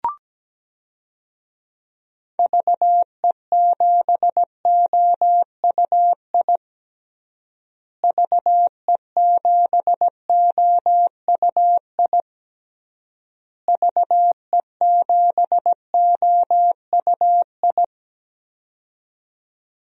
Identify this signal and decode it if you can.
Morse code, heard as a keyed tone, about 17 wpm, 700 Hz. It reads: VE7OUI VE7OUI VE7OUI
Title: DXCC-100 Call Signs - 6 Characters - Repeated 3x - 17wpm